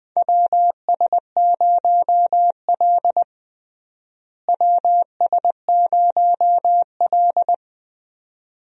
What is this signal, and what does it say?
Morse code, heard as a keyed tone, about 20 wpm, 700 Hz. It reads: WS0L WS0L
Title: USA Call Signs - 4 Characters - Repeated 3x - 20wpm